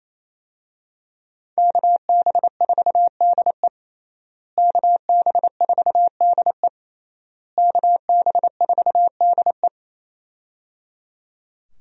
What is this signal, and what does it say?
Morse code, heard as a keyed tone, about 27 wpm, 700 Hz. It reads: KB4DE KB4DE KB4DE
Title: USA Call Signs - 5 Characters - Repeated 3x - 28wpm